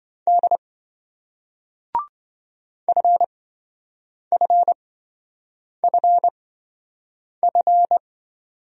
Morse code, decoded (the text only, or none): D FFFF